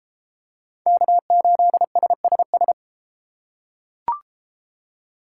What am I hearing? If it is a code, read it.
Morse code, heard as a keyed tone, about 31 wpm, 700 Hz. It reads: K8SSS